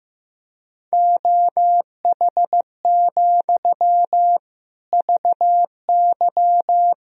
Morse code, heard as a keyed tone, about 15 wpm, 700 Hz. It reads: OH, VY
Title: Sentences from Top 500 Words and Abbreviations 15wpm